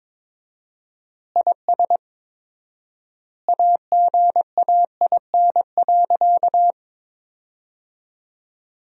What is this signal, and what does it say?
Morse code, heard as a keyed tone, about 22 wpm, 700 Hz. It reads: IS AGAIN.
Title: 4x Word-Spacing - Sentences from Top 200 Words - 22wpm